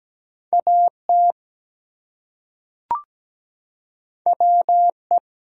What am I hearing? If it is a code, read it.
Morse code, heard as a keyed tone, about 17 wpm, 700 Hz. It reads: AT WE
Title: Two Letter Words - 17wpm